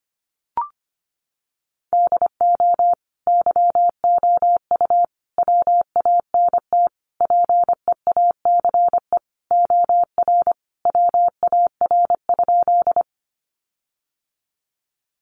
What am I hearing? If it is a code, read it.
Morse code, heard as a keyed tone, about 25 wpm, 700 Hz. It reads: DO YOU WANT PEACE OR WAR?
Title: Sentences from Top 1500 Words 25wpm